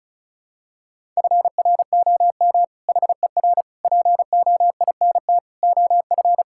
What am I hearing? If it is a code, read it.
Morse code, heard as a keyed tone, about 36 wpm, 700 Hz. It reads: FROM HER POINT OF
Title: Sentences from Top 1000 Words - Review Part 6 - 35wpm